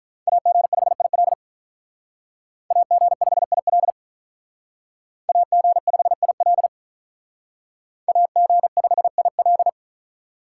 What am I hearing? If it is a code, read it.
Morse code, heard as a keyed tone, about 45 wpm, 700 Hz. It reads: AG5IL AG5IL AG5IL AG5IL